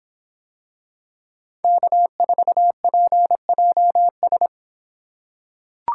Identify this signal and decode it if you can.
Morse code, heard as a keyed tone, about 24 wpm, 700 Hz. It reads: K4PJS